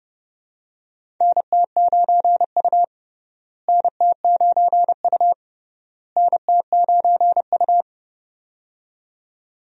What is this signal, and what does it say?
Morse code, heard as a keyed tone, about 32 wpm, 700 Hz. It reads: NT9U NT9U NT9U
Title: USA Call Signs - 4 Characters - Repeated 3x - 30wpm